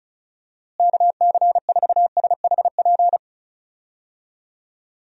Morse code, heard as a keyed tone, about 36 wpm, 700 Hz. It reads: KC4SHP
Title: USA Call Signs - 6 Characters - Repeated 3x - 35wpm